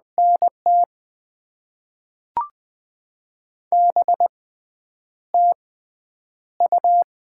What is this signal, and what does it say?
Morse code, heard as a keyed tone, about 20 wpm, spaced as for 7 wpm, 700 Hz. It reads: N T B TU
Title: Top 20 QSO Elements - 6x Character Spacing - 20wpm